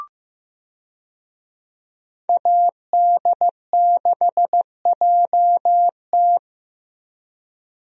Morse code, heard as a keyed tone, about 15 wpm, 700 Hz. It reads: AD6JT
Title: USA Call Signs - 5 Characters - Repeated 2x - 15wpm